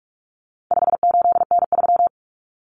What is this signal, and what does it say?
Morse code, heard as a keyed tone, about 44 wpm, 700 Hz. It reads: 58N3
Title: Common Cut Numbers - 4-Digit 45wpm